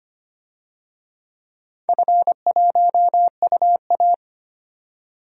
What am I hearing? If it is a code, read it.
Morse code, heard as a keyed tone, about 25 wpm, 700 Hz. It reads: F1UA
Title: DXCC-100 Call Signs - 4 Characters - 25wpm